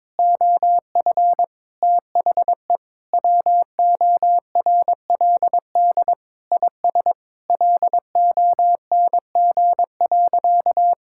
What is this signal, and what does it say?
Morse code, heard as a keyed tone, about 22 wpm, 700 Hz. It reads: OF THE WORLD IS LONG.
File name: practice / 22wpm / Sentences with 3-Word Phrases - 22wpm